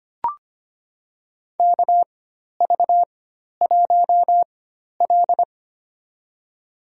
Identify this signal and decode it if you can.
Morse code, heard as a keyed tone, about 25 wpm, spaced as for 12 wpm, 690 Hz. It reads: KV1L
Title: USA Call Signs - 4 Characters - 4x Character Spacing - 25wpm